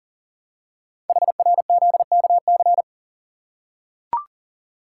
Morse code, heard as a keyed tone, about 42 wpm, 690 Hz. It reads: HRZKC
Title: Code Groups - 5 Letters - 40wpm